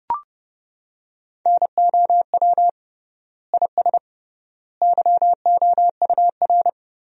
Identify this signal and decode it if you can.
Morse code, heard as a keyed tone, about 30 wpm, 700 Hz. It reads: NOW IS YOUR
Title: 3x Word-Spacing - Sentences from Top 100 Words - 30wpm